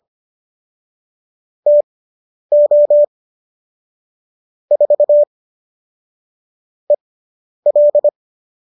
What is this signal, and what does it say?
Morse code, heard as a keyed tone, about 25 wpm, spaced as for 10 wpm, 570 Hz. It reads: TO 4 EL